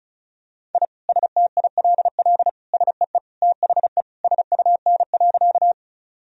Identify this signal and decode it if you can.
Morse code, heard as a keyed tone, about 36 wpm, 700 Hz. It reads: I STILL SEE THE SUN.
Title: Sentences from Top 1000 Words - Review Part 1 - 35wpm